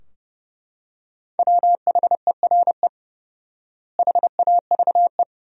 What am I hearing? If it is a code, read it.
Morse code, heard as a keyed tone, about 30 wpm, 700 Hz. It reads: WHERE HAVE
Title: 4x Word-Spacing - Sentences from Top 200 Words - 30wpm